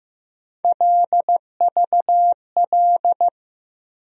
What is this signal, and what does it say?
Morse code, heard as a keyed tone, about 15 wpm, 700 Hz. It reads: LVL
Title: USA Call Sign Postfixes - 3 Letters 15wpm